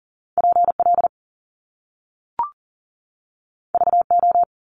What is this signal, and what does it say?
Morse code, heard as a keyed tone, about 41 wpm, 700 Hz. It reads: PL VO
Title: Top Two Letter Combinations in Words 40wpm